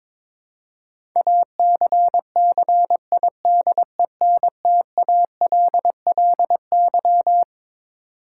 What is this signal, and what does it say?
Morse code, heard as a keyed tone, about 22 wpm, 700 Hz. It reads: ACCIDENTALLY
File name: Twelve Letter Words - 22wpm